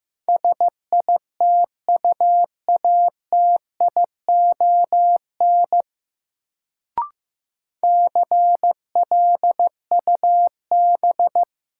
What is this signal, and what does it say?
Morse code, heard as a keyed tone, about 15 wpm, 700 Hz. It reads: SITUATION CLUB
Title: Sentences from Top 1200 Words - New Word Drill Alt - 15wpm